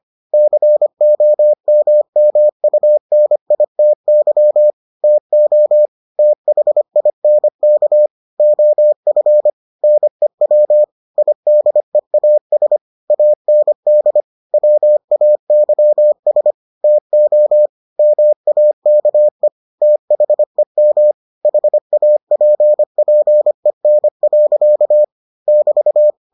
Morse code, heard as a keyed tone, about 25 wpm, 600 Hz. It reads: COMMUNITY TO THINK OF NEW IDEAS AND WAYS TO MAKE THEM HAPPEN. <BT>